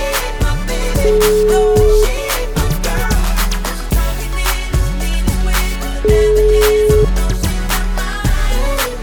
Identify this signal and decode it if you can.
Ringback tone (Europe, 425 Hz), with noise.